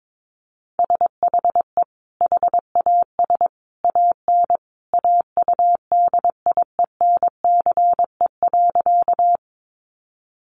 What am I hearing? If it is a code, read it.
Morse code, heard as a keyed tone, about 22 wpm, 700 Hz. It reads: SHE HAS AN AUDIENCE.
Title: Sentences from Top 800 Words 22wpm